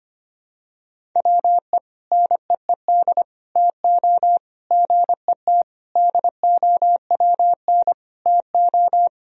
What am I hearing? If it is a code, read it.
Morse code, heard as a keyed tone, about 25 wpm, 700 Hz. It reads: WE NEED TO GET DOWN TO
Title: Sentences from Top 1000 Words - Review Part 1 - 25wpm